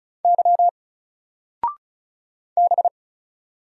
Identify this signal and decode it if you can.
Morse code, heard as a keyed tone, about 35 wpm, 700 Hz. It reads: Y B
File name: Single-Letter-Number-35wpm